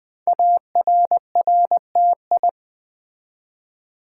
Morse code, heard as a keyed tone, about 20 wpm, 700 Hz. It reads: ARRTI
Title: Code Groups - 5 Letters - 20wpm